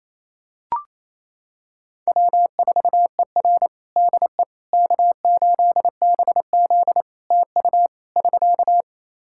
Morse code, heard as a keyed tone, about 27 wpm, 700 Hz. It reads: W4ER DE K8BZ TU <SK>
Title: Practice QSO - Exchange 4 - Ending - 28wpm